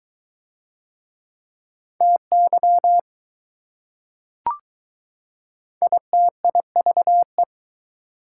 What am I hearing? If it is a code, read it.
Morse code, heard as a keyed tone, about 21 wpm, 700 Hz. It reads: TY ITIVE